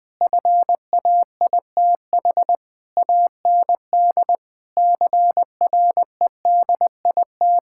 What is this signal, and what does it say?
Morse code, heard as a keyed tone, about 20 wpm, 700 Hz. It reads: FAITH AND CREDIT
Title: Sentences with 9-Word Phrases - 20wpm